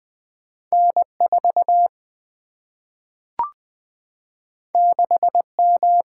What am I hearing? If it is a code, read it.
Morse code, heard as a keyed tone, about 20 wpm, 700 Hz. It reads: N 4 6 M
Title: Single-Letter Single-Number 20wpm